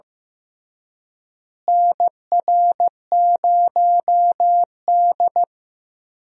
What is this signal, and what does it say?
Morse code, heard as a keyed tone, about 15 wpm, 700 Hz. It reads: NR0D
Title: USA Call Signs - 4 Characters - Repeated 3x - 15wpm